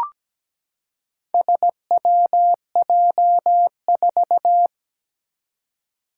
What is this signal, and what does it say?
Morse code, heard as a keyed tone, about 17 wpm, 700 Hz. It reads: SWJ4
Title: Four Challenging Letter-Number Combinations 17wpm